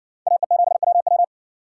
Morse code, heard as a keyed tone, about 60 wpm, 700 Hz. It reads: AE7WW